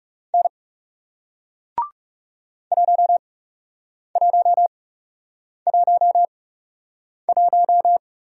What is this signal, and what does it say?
Morse code, heard as a keyed tone, about 37 wpm, spaced as for 8 wpm, 700 Hz. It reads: N 1111
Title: Speed-Racing Single Letter-Number 30wpm